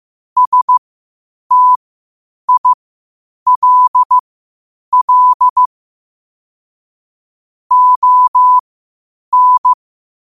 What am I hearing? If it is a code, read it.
Morse code, heard as a keyed tone, about 15 wpm, spaced as for 8 wpm, 1000 Hz. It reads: STILL ON
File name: qotdfiles / 8wpm